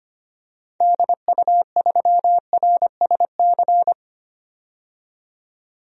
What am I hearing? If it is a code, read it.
Morse code, heard as a keyed tone, about 25 wpm, 700 Hz. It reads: DU3RSC
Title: DXCC-100 Call Signs - 6 Characters - 25wpm